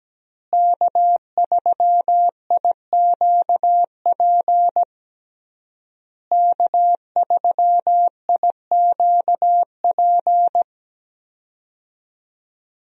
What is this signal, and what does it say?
Morse code, heard as a keyed tone, about 17 wpm, 700 Hz. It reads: K3IQP K3IQP
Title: USA Call Signs - 5 Characters - Repeated 3x - 17wpm